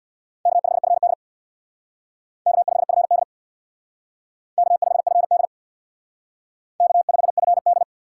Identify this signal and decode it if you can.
Morse code, heard as a keyed tone, about 65 wpm, 700 Hz. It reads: K5FD K5FD K5FD K5FD